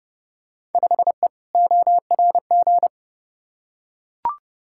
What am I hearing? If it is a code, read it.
Morse code, heard as a keyed tone, about 30 wpm, 700 Hz. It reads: 5E ORG
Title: Field Day Exchanges Encore - 30wpm